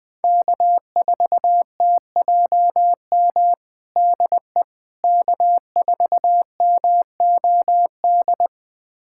Morse code, heard as a keyed tone, about 20 wpm, 700 Hz. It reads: K4TJM DE K4MOD
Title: USA Callsign DE Callsign - 5 Characters - 20wpm